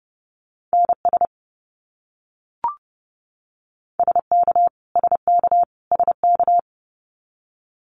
Morse code, heard as a keyed tone, about 30 wpm, 700 Hz. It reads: NS SK SK SK
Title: Canadian Provinces and Territories - Repeated 3x - 30wpm